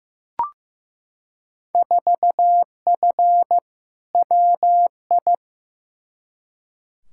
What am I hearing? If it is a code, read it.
Morse code, heard as a keyed tone, about 15 wpm, 700 Hz. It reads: 4F WI
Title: Field Day Exchanges - 15wpm